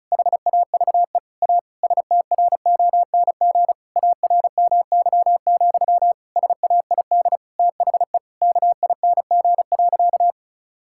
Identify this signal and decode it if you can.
Morse code, heard as a keyed tone, about 36 wpm, 700 Hz. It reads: HAVE A STRONG ARMY, SAID THE KING.
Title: Sentences from Top 1000 Words - Review Part 5 - 35wpm